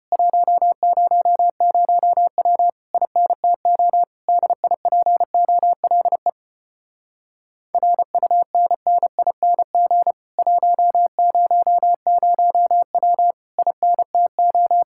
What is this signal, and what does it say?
Morse code, heard as a keyed tone, about 31 wpm, 700 Hz. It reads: 100W INTO DIPOLE RUNNING 100W INTO
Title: Speed-Racing Common QSO Phrases - 25wpm